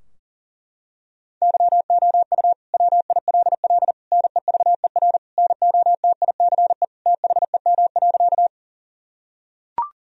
Morse code, heard as a keyed tone, about 40 wpm, 700 Hz. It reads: YOU WILL NEVER NOTICE THEM.
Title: Sentences from Top 1000 Words - Review Part 2 - 40wpm